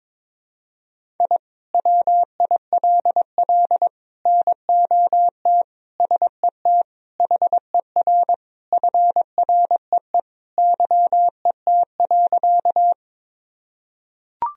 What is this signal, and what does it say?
Morse code, heard as a keyed tone, about 22 wpm, 700 Hz. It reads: I WILL NOT SET HER FREE YET.
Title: Sentences from Top 1000 Words - Review Part 1 - 22wpm